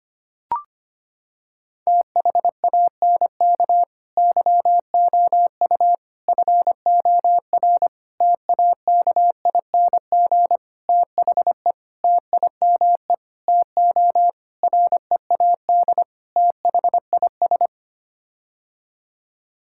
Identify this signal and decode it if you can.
Morse code, heard as a keyed tone, about 25 wpm, 700 Hz. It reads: THANK YOU FOR TAKING THE TIME TO READ THIS
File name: Sets of 9 Words - 25wpm